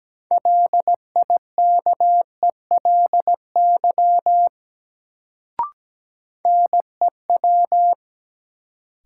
Morse code, heard as a keyed tone, about 17 wpm, 700 Hz. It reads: LIKELY NEW
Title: Sets of 2 Words - New Word Drill - 17wpm